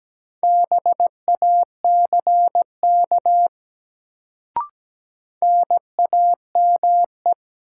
Morse code, heard as a keyed tone, about 17 wpm, 700 Hz. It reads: BACK NAME
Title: Sentences from Top 200 Words - New Word Drill - 17wpm